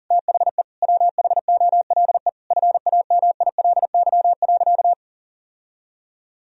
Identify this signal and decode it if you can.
Morse code, heard as a keyed tone, about 37 wpm, 700 Hz. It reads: THE WHOLE FAMILY.
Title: Sentences from Top 1000 Words - Review Part 1 - 40wpm